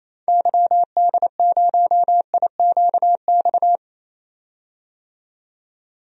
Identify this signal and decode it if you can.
Morse code, heard as a keyed tone, about 28 wpm, 700 Hz. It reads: YD0IQX
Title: DXCC-100 Call Signs - 6 Characters - 28wpm